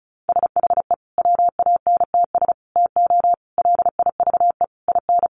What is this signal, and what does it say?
Morse code, heard as a keyed tone, about 34 wpm, 700 Hz. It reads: SHE WANTS TO LIVE IN
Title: Easy Phrases 35wpm